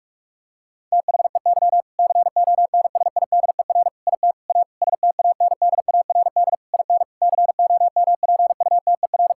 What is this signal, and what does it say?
Morse code, heard as a keyed tone, about 46 wpm, 700 Hz. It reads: THEY CONSIDER IT A STANDARD IN COMPUTER
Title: Sentences from Top 1000 Words - Review Part 6 - 45wpm